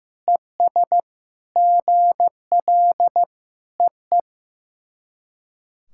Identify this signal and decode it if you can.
Morse code, heard as a keyed tone, about 14 wpm, 700 Hz. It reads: ES GL EE
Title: SOTA QSO Practice 15wpm